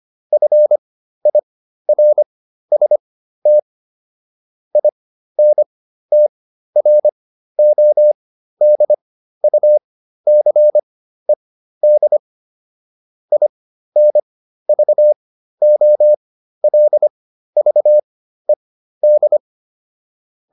Morse code, heard as a keyed tone, about 25 wpm, spaced as for 13 wpm, 600 Hz. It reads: FIRST INTRODUCED INVOLVED